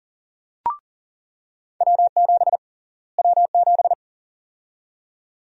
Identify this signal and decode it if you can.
Morse code, heard as a keyed tone, about 38 wpm, 700 Hz. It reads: W7 W7
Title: USA Call Sign Prefixes - Repeated 2x - 40wpm